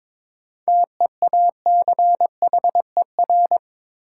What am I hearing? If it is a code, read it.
Morse code, heard as a keyed tone, about 22 wpm, 700 Hz. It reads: TEACHER
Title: Sentences from Top 900 Words - New Word Drill - 22wpm